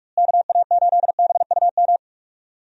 Morse code, heard as a keyed tone, about 44 wpm, 700 Hz. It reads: KA8BUM
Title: USA Call Signs - 6 Characters - Repeated 3x - 45wpm